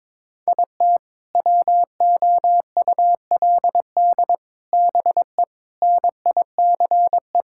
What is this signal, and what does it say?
Morse code, heard as a keyed tone, about 22 wpm, 700 Hz. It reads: IT WOULD BE NICE
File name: Sentences with 7-Word Phrases - 22wpm